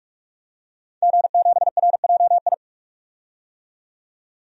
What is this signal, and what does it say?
Morse code, heard as a keyed tone, about 44 wpm, 700 Hz. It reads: G7RJI
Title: DXCC-100 Call Signs - 5 Characters - 45wpm